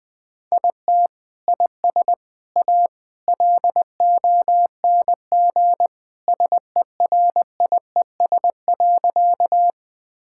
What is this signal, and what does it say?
Morse code, heard as a keyed tone, about 20 wpm, 700 Hz. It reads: IT IS A LONG SERIES.
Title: Sentences from Top 1000 Words - Review Part 5 - 20wpm